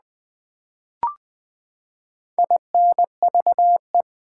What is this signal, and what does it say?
Morse code, heard as a keyed tone, about 20 wpm, 700 Hz. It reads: INVE
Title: Build A Word - 4 Letters At A Time - 12-Letter Words - 20wpm